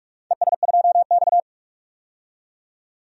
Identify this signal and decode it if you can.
Morse code, heard as a keyed tone, about 46 wpm, 700 Hz. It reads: ES2X
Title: DXCC-100 Call Signs - 4 Characters - 45wpm